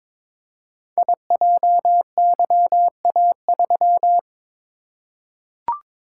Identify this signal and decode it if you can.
Morse code, heard as a keyed tone, about 22 wpm, 700 Hz. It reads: IJYA3
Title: Code Groups - 5 Letter:Numbers - 22wpm